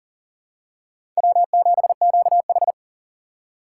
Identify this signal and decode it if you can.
Morse code, heard as a keyed tone, about 40 wpm, 700 Hz. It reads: W7QH
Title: USA Call Signs - 4 Characters - 6x Character Spacing - 40wpm